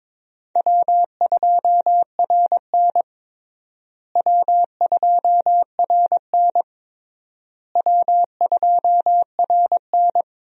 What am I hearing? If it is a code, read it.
Morse code, heard as a keyed tone, about 22 wpm, 700 Hz. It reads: W2RN W2RN W2RN
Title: USA Call Signs - 4 Characters - Repeated 3x - 22wpm